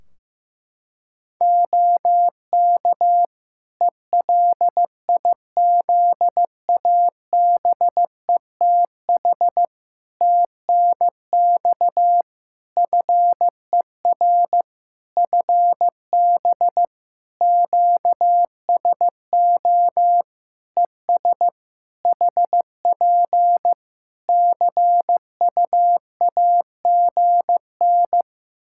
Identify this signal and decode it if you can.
Morse code, heard as a keyed tone, about 15 wpm, 700 Hz. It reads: OK ELIZABETH TNX FER FB QSO ES HP CUAGN
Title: Practice QSO - Exchange 4 - Ending - 15wpm